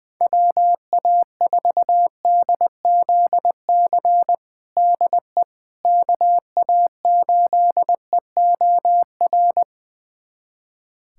Morse code, heard as a keyed tone, about 21 wpm, 700 Hz. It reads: WA4DZC DE KA8EOR
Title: USA Callsign DE Callsign - 6 Characters - 20wpm